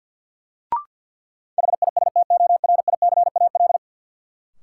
Morse code, heard as a keyed tone, about 51 wpm, 700 Hz. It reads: HISTORICAL
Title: Sentences from Top 600 Words - New Word Drill - 50wpm